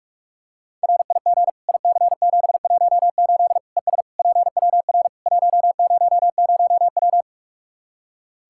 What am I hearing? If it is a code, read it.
Morse code, heard as a keyed tone, about 46 wpm, 700 Hz. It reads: RIG IC718 ES PWR 100W
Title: Practice QSO - Exchange 2 - Info - 45wpm